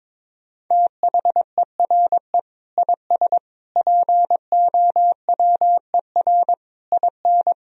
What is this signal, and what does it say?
Morse code, heard as a keyed tone, about 22 wpm, 700 Hz. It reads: THERE IS POWER IN
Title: Sentences from Top 1000 Words - Review Part 4 - 22wpm